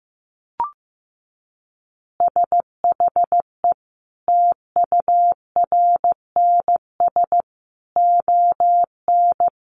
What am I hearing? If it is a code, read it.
Morse code, heard as a keyed tone, about 15 wpm, 700 Hz. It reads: SHE TURNS ON
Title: Additional Easy Phrases 15wpm